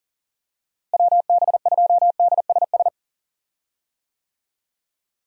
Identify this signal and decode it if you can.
Morse code, heard as a keyed tone, about 40 wpm, 700 Hz. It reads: WB2DSS
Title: USA Call Signs - 6 Characters - Repeated 2x - 40wpm